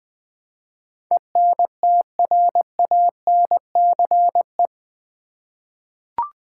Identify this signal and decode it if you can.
Morse code, heard as a keyed tone, about 20 wpm, 700 Hz. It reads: ENTRANCE
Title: Eight Letter Words - 20wpm